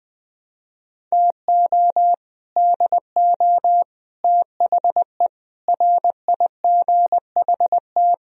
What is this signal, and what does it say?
Morse code, heard as a keyed tone, about 21 wpm, 700 Hz. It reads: TO DO THE RIGHT